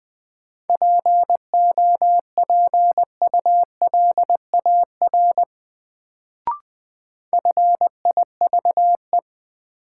Morse code, heard as a keyed tone, about 20 wpm, 700 Hz. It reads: POPULAR FIVE
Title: Sets of 5 Words - New Word Drill - 20wpm